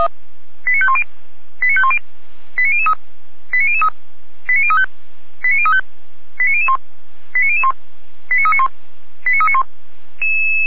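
Five-tone selective call (ZVEI), heard as quick runs of five tones, tones of about 70 ms. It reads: 89510 89510 89003 89003 89035 89035 89001 89001 89391 89391
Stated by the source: ZVEI five-tone selective call received by SDR, code 89042.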